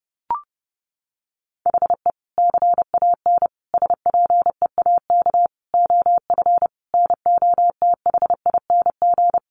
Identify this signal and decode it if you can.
Morse code, heard as a keyed tone, about 30 wpm, 700 Hz. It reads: HE CAN SPEAK OF NOTHING
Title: Sentences from Top 1100 Words 30wpm